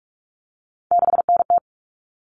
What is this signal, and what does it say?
Morse code, heard as a keyed tone, about 44 wpm, 700 Hz. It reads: 6NT
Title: All Cut Numbers - 3-Digit 45wpm